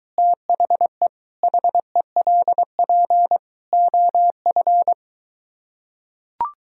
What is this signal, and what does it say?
Morse code, heard as a keyed tone, about 23 wpm, 700 Hz. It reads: THE HELP OF